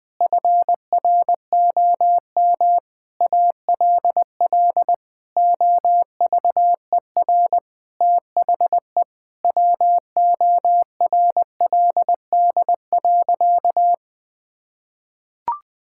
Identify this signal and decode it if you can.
Morse code, heard as a keyed tone, about 20 wpm, 700 Hz. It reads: FROM ALL OVER THE WORLD.
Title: Sentences with 5-Word Phrases - 20wpm